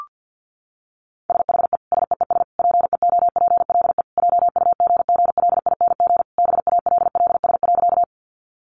Morse code, heard as a keyed tone, about 50 wpm, 700 Hz. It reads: SHE SEES PEOPLE JUGGLING BALLS.
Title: Easy Phrases 50wpm